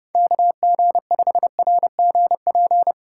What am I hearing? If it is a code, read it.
Morse code, heard as a keyed tone, about 32 wpm, 700 Hz. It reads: KG5RGP